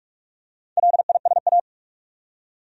Morse code, heard as a keyed tone, about 45 wpm, 700 Hz. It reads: LISA
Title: USA Common Names - Max 4 Letters - 45wpm